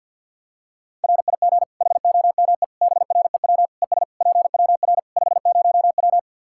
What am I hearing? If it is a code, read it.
Morse code, heard as a keyed tone, about 48 wpm, 700 Hz. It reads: RIG HOME BREW ES PWR 50W
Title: Practice QSO - Exchange 2 - Info - 50wpm